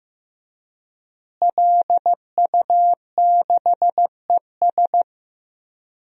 Morse code, heard as a keyed tone, about 15 wpm, 700 Hz. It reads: LU6ES
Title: DXCC-100 Call Signs - 5 Characters - Repeated 2x - 15wpm